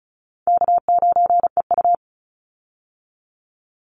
Morse code, heard as a keyed tone, about 34 wpm, 700 Hz. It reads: K9EU
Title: Easy Call Signs Repeated 2x - 35wpm